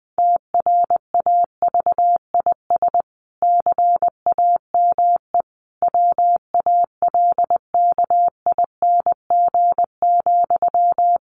Morse code, heard as a keyed tone, about 20 wpm, 700 Hz. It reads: TRAVIS CAME WALKING,
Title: A Sound Of Thunder 20wpm